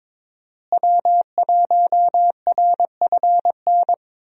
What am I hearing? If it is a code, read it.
Morse code, heard as a keyed tone, about 22 wpm, 700 Hz. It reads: W1RFN
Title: USA Call Signs - 5 Characters - Repeated 3x - 22wpm